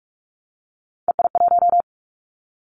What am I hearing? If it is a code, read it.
Morse code, heard as a keyed tone, about 44 wpm, 700 Hz. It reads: EI1
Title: Callsign Prefixes for Top 50 DXCC - 45wpm